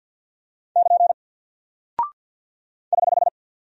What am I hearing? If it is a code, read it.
Morse code, heard as a keyed tone, about 49 wpm, 700 Hz. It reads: <KN> <HH>
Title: Single Character (Letter:Number:Punctuation:Prosign) Alt 50wpm